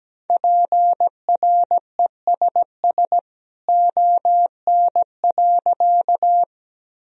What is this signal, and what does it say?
Morse code, heard as a keyed tone, about 17 wpm, 700 Hz. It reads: PRESS ON.
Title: Sentences from Top 1000 Words - Review Part 1 - 17wpm